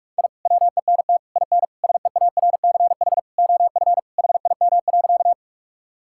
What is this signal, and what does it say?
Morse code, heard as a keyed tone, about 46 wpm, 700 Hz. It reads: I WENT IN SEARCH OF HIM.
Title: Sentences from Top 1000 Words - Review Part 5 - 45wpm